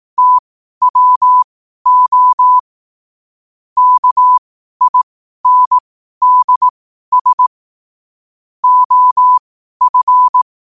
Morse code, heard as a keyed tone, about 18 wpm, spaced as for 12 wpm, 1000 Hz. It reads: TWO KINDS OF
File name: qotdfiles / 12wpm